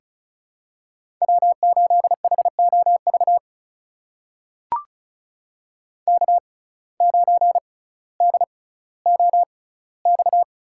Morse code, heard as a keyed tone, about 36 wpm, spaced as for 18 wpm, 700 Hz. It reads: W8HOV K9DOX
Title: USA Call Signs - 5 Characters - 6x Character Spacing - 35wpm